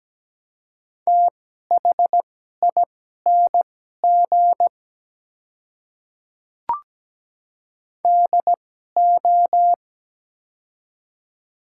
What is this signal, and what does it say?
Morse code, heard as a keyed tone, about 17 wpm, spaced as for 12 wpm, 700 Hz. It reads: THING DO
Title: 2x Character Spacing - Top 100 Words - 17wpm